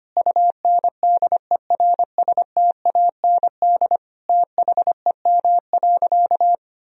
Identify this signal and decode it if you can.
Morse code, heard as a keyed tone, about 25 wpm, 700 Hz. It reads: UNDERSTAND THEM.
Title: Sentences from Top 1000 Words - Review Part 2 - 25wpm